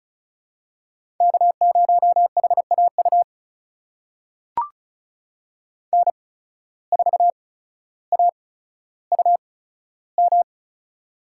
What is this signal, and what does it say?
Morse code, heard as a keyed tone, about 36 wpm, spaced as for 15 wpm, 700 Hz. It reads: K0HAU N4AUM
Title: USA Call Signs - 5 Characters - 8x Character Spacing - 35wpm